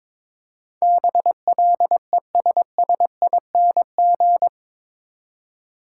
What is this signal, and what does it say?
Morse code, heard as a keyed tone, about 22 wpm, 700 Hz. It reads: BLESSING
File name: practice / 22wpm / Eight Letter Words - 22wpm